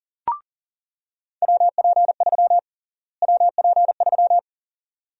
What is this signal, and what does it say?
Morse code, heard as a keyed tone, about 40 wpm, 700 Hz. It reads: WP3 WP3
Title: USA Call Sign Prefixes - Repeated 2x - 40wpm